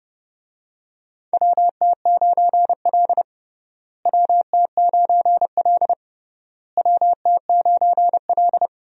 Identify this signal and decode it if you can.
Morse code, heard as a keyed tone, about 30 wpm, 700 Hz. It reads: WT9L WT9L WT9L
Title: USA Call Signs - 4 Characters - Repeated 3x - 30wpm